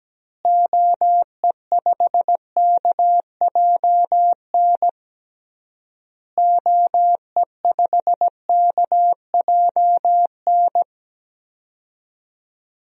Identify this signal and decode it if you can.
Morse code, heard as a keyed tone, about 17 wpm, 700 Hz. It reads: OE5KJN OE5KJN
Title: DXCC-100 Call Signs - 6 Characters - Repeated 3x - 17wpm